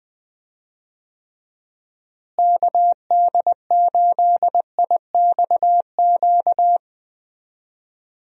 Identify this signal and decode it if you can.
Morse code, heard as a keyed tone, about 21 wpm, 700 Hz. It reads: KD8IXQ
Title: USA Call Signs - 6 Characters - Repeated 2x - 20wpm